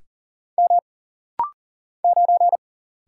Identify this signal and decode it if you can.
Morse code, heard as a keyed tone, about 40 wpm, 700 Hz. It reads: M 9
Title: Single Letter-Number - Rapid-Fire 40wpm